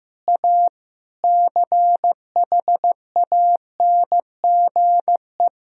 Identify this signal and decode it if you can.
Morse code, heard as a keyed tone, about 15 wpm, 700 Hz. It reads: A CHANGE
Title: Sentences from Top 1000 Words - Review Part 1 - 15wpm